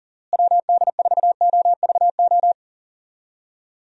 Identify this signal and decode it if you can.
Morse code, heard as a keyed tone, about 37 wpm, 700 Hz. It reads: WD4OVO
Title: USA Call Signs - 6 Characters - Repeated 2x - 40wpm